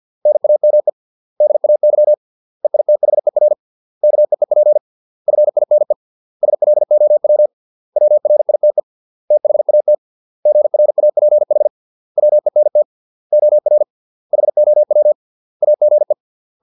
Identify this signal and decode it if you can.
Morse code, heard as a keyed tone, about 50 wpm, 600 Hz. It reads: NAME DAY EITHER KEEP FINE SLOW WRITE THAT GRAPH WENT OR HOW AGE